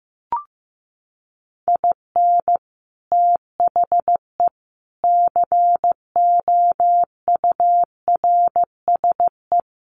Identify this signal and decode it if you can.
Morse code, heard as a keyed tone, about 15 wpm, 700 Hz. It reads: IN THE COURSE
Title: Sets of 4 Words 15wpm